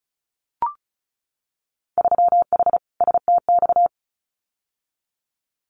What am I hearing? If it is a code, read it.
Morse code, heard as a keyed tone, about 36 wpm, 700 Hz. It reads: 3H STX
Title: Winter Field Day Exchanges 35wpm